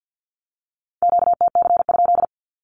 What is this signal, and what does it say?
Morse code, heard as a keyed tone, about 50 wpm, 700 Hz. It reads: QTC?
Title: Q-Codes with Spoken Meaning - 50wpm